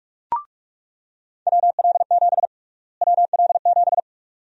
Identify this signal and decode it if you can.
Morse code, heard as a keyed tone, about 41 wpm, 700 Hz. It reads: WL7 WL7
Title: USA Call Sign Prefixes - Repeated 2x - 45wpm